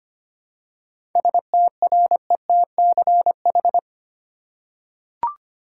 Morse code, heard as a keyed tone, about 25 wpm, 700 Hz. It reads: STRETCH
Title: Seven Letter Words - 25wpm